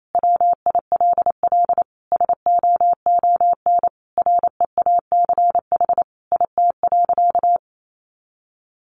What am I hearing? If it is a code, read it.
Morse code, heard as a keyed tone, about 28 wpm, 700 Hz. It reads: WILL SOON REACH IT.
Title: Sentences from Top 400 Words 28wpm